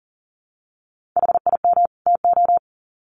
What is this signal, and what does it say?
Morse code, heard as a keyed tone, about 37 wpm, 700 Hz. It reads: HIM TO
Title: Sets of 2 Words 40wpm